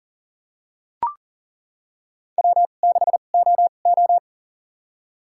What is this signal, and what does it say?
Morse code, heard as a keyed tone, about 40 wpm, spaced as for 29 wpm, 700 Hz. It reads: W6OO
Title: USA Call Signs - 4 Characters - 2x Character Spacing - 40wpm